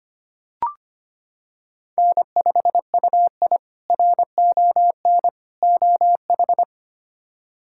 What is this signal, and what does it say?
Morse code, heard as a keyed tone, about 25 wpm, 700 Hz. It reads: N5UI RON OH
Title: K1USN SST QSO Practice - 25wpm